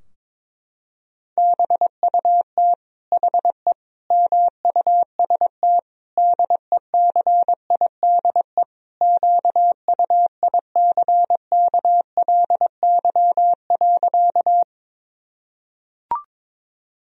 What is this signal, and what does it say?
Morse code, heard as a keyed tone, about 22 wpm, 700 Hz. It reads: BUT HE MUST DECIDE QUICKLY.
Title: Sentences from Top 1000 Words - Review Part 3 - 22wpm